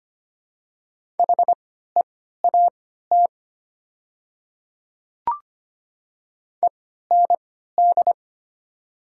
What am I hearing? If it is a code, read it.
Morse code, heard as a keyed tone, about 25 wpm, spaced as for 14 wpm, 700 Hz. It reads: HEAT END
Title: 3x Character Spacing - Top 500 Words - 25wpm